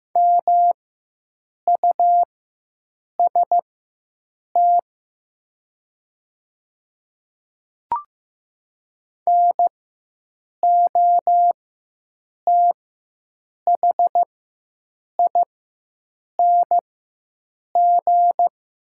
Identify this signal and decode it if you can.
Morse code, heard as a keyed tone, about 15 wpm, spaced as for 7 wpm, 700 Hz. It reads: MUST NOTHING
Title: 4x Character Spacing - Top 500 Words - 15wpm